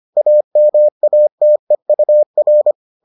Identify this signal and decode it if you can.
Morse code, heard as a keyed tone, about 25 wpm, 600 Hz. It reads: AMATEUR